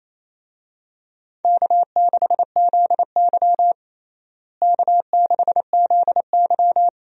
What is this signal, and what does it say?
Morse code, heard as a keyed tone, about 28 wpm, 700 Hz. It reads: K6ZY K6ZY
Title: USA Call Signs - 4 Characters - Repeated 2x - 28wpm